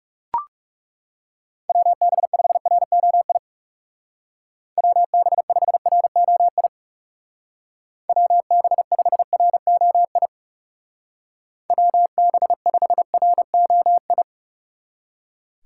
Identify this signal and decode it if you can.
Morse code, heard as a keyed tone, about 35 wpm, 700 Hz. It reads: WB5ROI WB5ROI WB5ROI WB5ROI
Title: Speed Racing - USA 2x3 Call Signs 30wpm